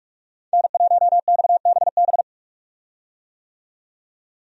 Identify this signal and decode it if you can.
Morse code, heard as a keyed tone, about 46 wpm, 700 Hz. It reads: N1XBB
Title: USA Call Signs - 5 Characters - 45wpm